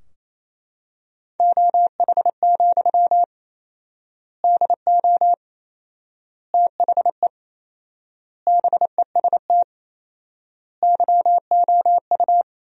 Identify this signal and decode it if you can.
Morse code, heard as a keyed tone, about 28 wpm, 700 Hz. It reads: OH, DO THE BEST YOU
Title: 4x Word-Spacing - Sentences from Top 500 Words - 28wpm